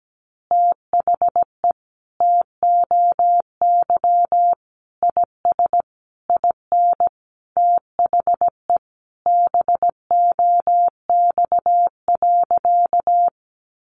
Morse code, heard as a keyed tone, about 17 wpm, 700 Hz. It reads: THE TOY IS IN THE BOX.
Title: Easy Phrases 17wpm